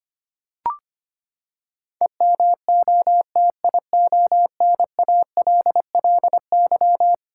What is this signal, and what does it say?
Morse code, heard as a keyed tone, about 25 wpm, 700 Hz. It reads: EMOTIONALLY
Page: morse-code-ninja-practice-25wpm